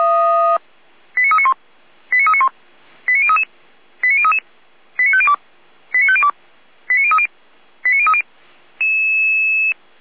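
Five-tone selective call (ZVEI), heard as quick runs of five tones, tones of about 70 ms. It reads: 89391 89391 89033 89033 89502 89502 89030 89030
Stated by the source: ZVEI five-tone selective call received on an SDR.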